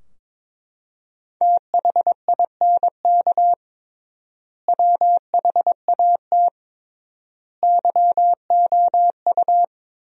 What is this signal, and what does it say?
Morse code, heard as a keyed tone, about 22 wpm, 700 Hz. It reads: THINK WHAT YOU
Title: 3x Word-Spacing - Sentences from Top 200 Words - 22wpm